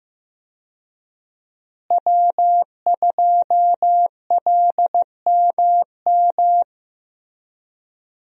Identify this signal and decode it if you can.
Morse code, heard as a keyed tone, about 15 wpm, 700 Hz. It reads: W2LMM
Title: USA Call Signs - 5 Characters - Repeated 2x - 15wpm